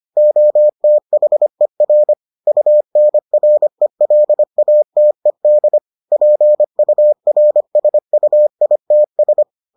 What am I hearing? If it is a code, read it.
Morse code, heard as a keyed tone, about 25 wpm, 600 Hz. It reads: OTHER UNRELATED PURSUITS